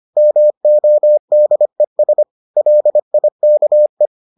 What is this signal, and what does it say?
Morse code, heard as a keyed tone, about 25 wpm, 600 Hz. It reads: MODES LIKE